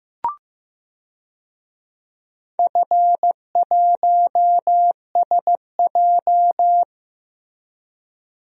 Morse code, heard as a keyed tone, about 15 wpm, 700 Hz. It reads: F1SJ
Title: DXCC-100 Call Signs - 4 Characters - 15wpm